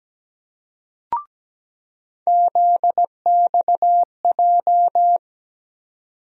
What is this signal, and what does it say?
Morse code, heard as a keyed tone, about 17 wpm, 700 Hz. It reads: ZXJ
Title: USA Call Sign Postfixes - 3 Letters 17wpm